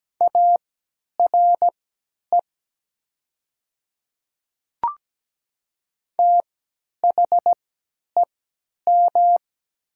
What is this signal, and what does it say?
Morse code, heard as a keyed tone, about 17 wpm, spaced as for 10 wpm, 700 Hz. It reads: ARE THEM
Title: 3x Character Spacing - Top 100 Words - 17wpm